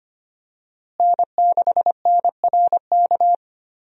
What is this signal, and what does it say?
Morse code, heard as a keyed tone, about 25 wpm, 700 Hz. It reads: N6NRK
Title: USA Call Signs - 5 Characters - Repeated 2x - 25wpm